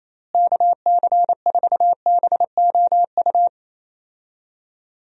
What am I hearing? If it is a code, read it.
Morse code, heard as a keyed tone, about 27 wpm, 700 Hz. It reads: KC4BOU
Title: USA Call Signs - 6 Characters - Repeated 2x - 28wpm